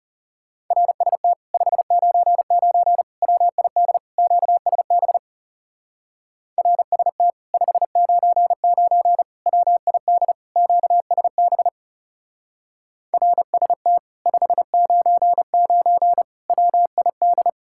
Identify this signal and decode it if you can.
Morse code, heard as a keyed tone, about 40 wpm, 700 Hz. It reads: RST 599 WID QSB RST 599 WID QSB RST 599 WID